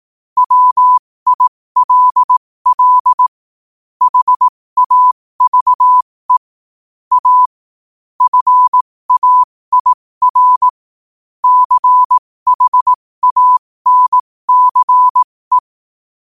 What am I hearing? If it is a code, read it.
Morse code, heard as a keyed tone, about 18 wpm, spaced as for 15 wpm, 1000 Hz. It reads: WILL HAVE A FAIR CHANCE
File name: 15wpm